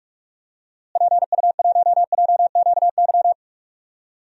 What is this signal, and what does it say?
Morse code, heard as a keyed tone, about 43 wpm, 700 Hz. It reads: PU1JQY
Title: DXCC-100 Call Signs - 6 Characters - Repeated 3x - 45wpm